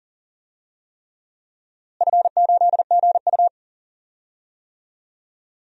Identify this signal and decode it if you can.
Morse code, heard as a keyed tone, about 40 wpm, 700 Hz. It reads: F8GU
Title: DXCC-100 Call Signs - 4 Characters - Repeated 3x - 40wpm